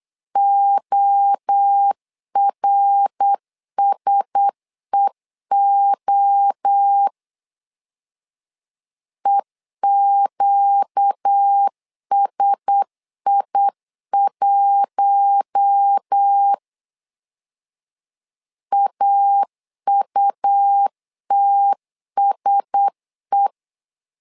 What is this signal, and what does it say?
Morse code, heard as a keyed tone, about 8 wpm, 800 Hz. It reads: ORSEO EQSI1 AUTSE
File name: LEZIONI / 4 Q1S 8wpm 2710